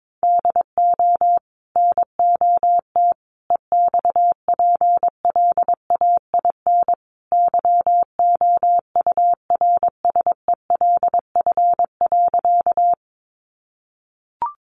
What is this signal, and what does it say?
Morse code, heard as a keyed tone, about 22 wpm, 700 Hz. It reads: DO NOT EXPLAIN YOURSELF.
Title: Sentences from Top 1100 Words 22wpm